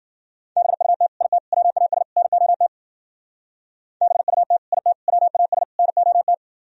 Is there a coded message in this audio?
Yes: Morse code from 0.6 to 6.3 s, heard as a keyed tone, about 51 wpm, 700 Hz. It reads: BUT IT WAS NOT BUT IT WAS NOT